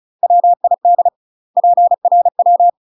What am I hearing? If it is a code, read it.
Morse code, heard as a keyed tone, about 34 wpm, 700 Hz. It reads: WID PRW
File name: UK MASTER QSOs TRAINING / 35WPM / UK MASTER QSOs 0037